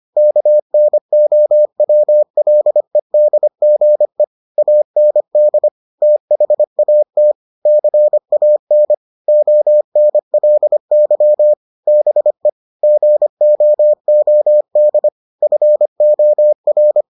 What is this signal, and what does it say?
Morse code, heard as a keyed tone, about 25 wpm, 600 Hz. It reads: KNOWLEDGE AND THAT CAN ONLY BE GOOD FOR